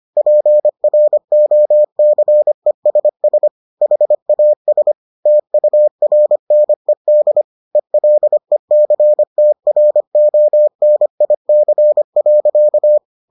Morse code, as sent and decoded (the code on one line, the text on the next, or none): .--. .-. --- -.-. ... ... / .... .- ... / - ..- .-. -. -.. / . .-.. -.-. - .-. --- -. .. -.-. .-.-.-
PROCESS HAS TURNED ELECTRONIC.